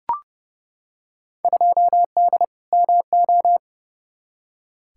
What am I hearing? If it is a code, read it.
Morse code, heard as a keyed tone, about 30 wpm, 700 Hz. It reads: 2D MO
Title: Field Day Exchanges - 30wpm